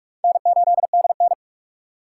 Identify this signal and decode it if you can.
Morse code, heard as a keyed tone, about 40 wpm, 700 Hz. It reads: N8DN